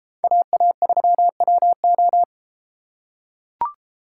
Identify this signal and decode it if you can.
Morse code, heard as a keyed tone, about 31 wpm, 700 Hz. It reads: AA3WO